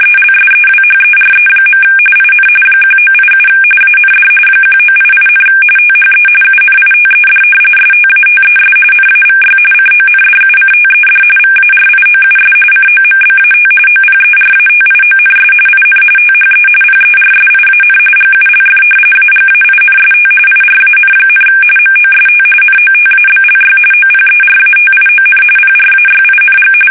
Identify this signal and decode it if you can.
RTTY, about 45.45 baud. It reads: ELSEWHERE. IN 1987 AND 1988, THE UNITED STATES NAVY CONDUCTED VARIOUS COMBAT OPERATIONS IN THE PERSIAN GULF AGAINST IRAN, MOST NOTABLY OPERATION PRAYING
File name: fsk850 / usnavy5